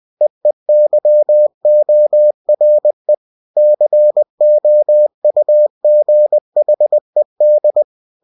Morse code, heard as a keyed tone, about 20 wpm, 600 Hz. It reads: EEYORE COUGHED
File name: cw-books / winnie-the-pooh / winnie-20wpm-page-0107